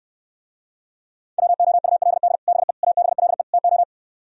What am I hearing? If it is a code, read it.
Morse code, heard as a keyed tone, about 67 wpm, 700 Hz. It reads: WOULD BE ABLE TO